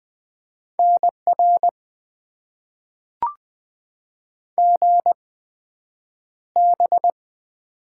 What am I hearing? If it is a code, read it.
Morse code, heard as a keyed tone, about 21 wpm, spaced as for 9 wpm, 700 Hz. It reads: NR GB
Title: Top 90 QSO Elements - 8x Character Spacing - 20wpm